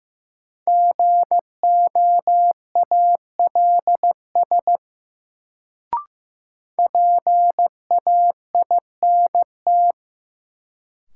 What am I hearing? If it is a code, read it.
Morse code, heard as a keyed tone, about 15 wpm, 700 Hz. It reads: GOALS PAINT
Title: Sentences from Top 1000 Words - Review Part 6 - New Word Drill - 15wpm